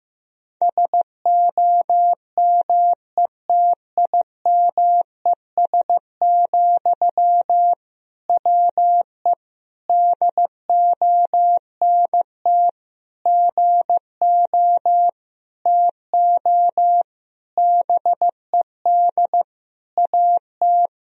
Morse code, heard as a keyed tone, about 15 wpm, 700 Hz. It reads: SOMETIMES, WE DONT GO TO BED AT
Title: Sentences from Top 1000 Words - Review Part 3 - 15wpm